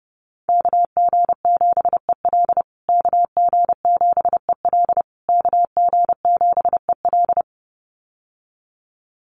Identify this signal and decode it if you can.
Morse code, heard as a keyed tone, about 30 wpm, 700 Hz. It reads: KG7EL KG7EL KG7EL
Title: USA Call Signs Repeated 3x 30wpm